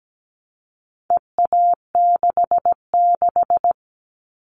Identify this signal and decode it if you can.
Morse code, heard as a keyed tone, about 17 wpm, 700 Hz. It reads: EA66
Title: All Cut Numbers - 4-Digit 17wpm